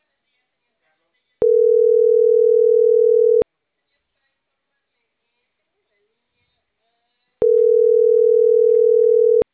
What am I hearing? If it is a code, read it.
Ringback tone (North America), 440 + 480 Hz, with speech.
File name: CENTRAL CAEA / out agcaea10